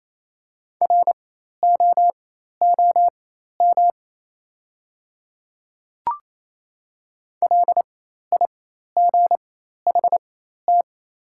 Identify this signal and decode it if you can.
Morse code, heard as a keyed tone, about 28 wpm, spaced as for 13 wpm, 700 Hz. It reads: ROOM LIGHT
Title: 4x Character Spacing - Top 300 Words - 28wpm